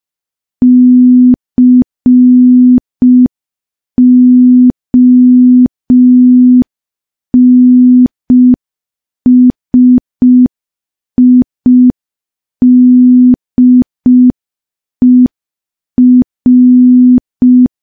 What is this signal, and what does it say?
Morse code, heard as a keyed tone, about 5 wpm, 250 Hz. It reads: CONSIDER
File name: morse / morse-training-05WPM